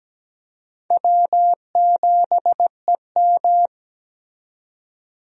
Morse code, heard as a keyed tone, about 17 wpm, 700 Hz. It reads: W7EM
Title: USA Call Signs - 4 Characters - 17wpm